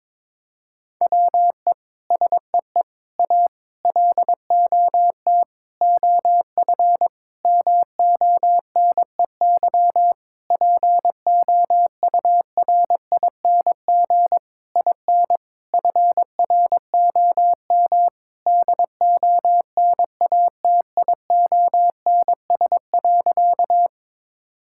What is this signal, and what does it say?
Morse code, heard as a keyed tone, about 22 wpm, 700 Hz. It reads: WE SEE A LOT OF MONEY POURING IN FROM DONATIONS.
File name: Sentences with 9-Word Phrases - 22wpm